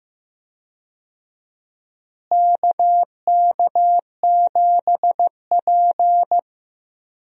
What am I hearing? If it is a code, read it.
Morse code, heard as a keyed tone, about 15 wpm, 700 Hz. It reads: KK7P
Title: USA Call Signs - 4 Characters - Repeated 2x - 15wpm